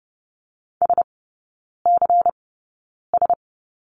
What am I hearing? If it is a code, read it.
Morse code, heard as a keyed tone, about 30 wpm, spaced as for 9 wpm, 690 Hz. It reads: SCS
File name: Lesson 17e - Letter u - Three Characters 30wpm